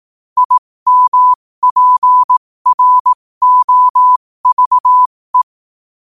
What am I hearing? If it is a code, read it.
Morse code, heard as a keyed tone, about 18 wpm, spaced as for 15 wpm, 1000 Hz. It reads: IMPROVE
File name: qotdfiles / 15wpm